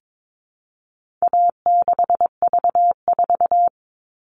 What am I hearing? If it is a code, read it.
Morse code, heard as a keyed tone, about 22 wpm, 700 Hz. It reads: A6V4
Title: All Cut Numbers - 4-Digit 22wpm